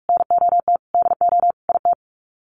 Morse code, heard as a keyed tone, about 44 wpm, 700 Hz. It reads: NOT DO IT
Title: Sets of 5 Words Encore2 45wpm